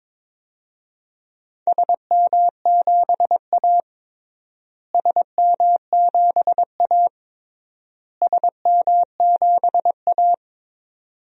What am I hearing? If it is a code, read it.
Morse code, heard as a keyed tone, about 21 wpm, 700 Hz. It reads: SM7A SM7A SM7A
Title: DXCC-100 Call Signs - 4 Characters - Repeated 3x - 22wpm